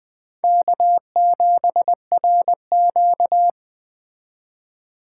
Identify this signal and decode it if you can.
Morse code, heard as a keyed tone, about 20 wpm, 700 Hz. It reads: K7RQ
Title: Speed Racing - USA 1x2 Call Signs 17wpm